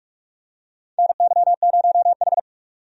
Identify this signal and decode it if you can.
Morse code, heard as a keyed tone, about 46 wpm, 700 Hz. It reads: NY0H
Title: USA Call Signs - 4 Characters - 45wpm